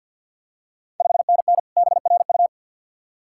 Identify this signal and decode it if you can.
Morse code, heard as a keyed tone, about 48 wpm, 690 Hz. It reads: 5NN BRU
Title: Ontario QSO Party Exchange 50wpm